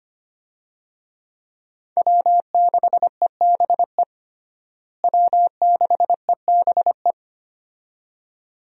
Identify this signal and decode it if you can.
Morse code, heard as a keyed tone, about 25 wpm, 700 Hz. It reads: W6EBE W6EBE
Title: USA Call Signs - 5 Characters - Repeated 2x - 25wpm